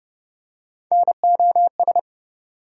Morse code, heard as a keyed tone, about 29 wpm, 700 Hz. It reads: NOS
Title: USA Call Sign Postfixes - 3 Letters 30wpm